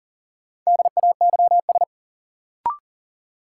Morse code, heard as a keyed tone, about 40 wpm, 700 Hz. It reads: DAYS
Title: Sentences from Top 1000 Words - Review Part 5 - New Word Drill - 40wpm